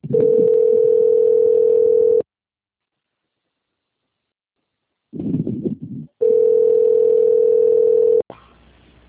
Ringback tone (North America), 440 + 480 Hz.
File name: out- - - - - .48675